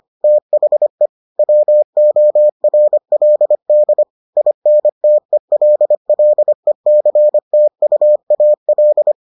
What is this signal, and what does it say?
Morse code, heard as a keyed tone, about 25 wpm, 600 Hz. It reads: THE WORLD INTELLECTUAL